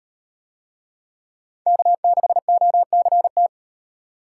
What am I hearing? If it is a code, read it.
Morse code, heard as a keyed tone, about 38 wpm, 700 Hz. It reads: K6OCT